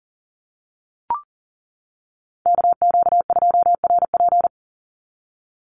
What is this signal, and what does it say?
Morse code, heard as a keyed tone, about 40 wpm, 700 Hz. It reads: KQ2RP
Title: NA SOTA Chaser Callsigns 40wpm